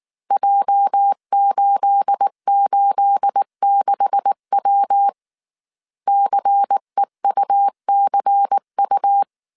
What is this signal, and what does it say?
Morse code, heard as a keyed tone, about 19 wpm, 800 Hz. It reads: J886W CEUCU
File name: CW / DET / 1 9 1500C S A 18wpm 2459